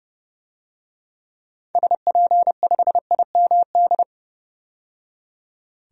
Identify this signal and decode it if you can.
Morse code, heard as a keyed tone, about 30 wpm, 700 Hz. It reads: SP5IMD
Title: DXCC-100 Call Signs - 6 Characters - 30wpm